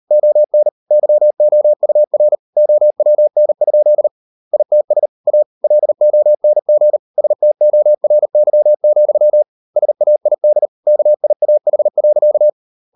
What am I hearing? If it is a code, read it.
Morse code, heard as a keyed tone, about 38 wpm, 600 Hz. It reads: ON YOUR OWN? ITS A LONG STORY, SAID KIAH.